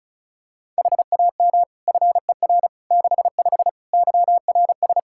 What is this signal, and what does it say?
Morse code, heard as a keyed tone, about 34 wpm, 700 Hz. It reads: HAM FER 65 YRS
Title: Practice QSO - Exchange 3 - More Info - 35wpm